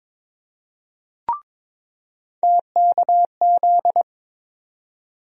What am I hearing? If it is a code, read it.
Morse code, heard as a keyed tone, about 22 wpm, 700 Hz. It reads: TKZ
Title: USA Call Sign Postfixes - 3 Letters 22wpm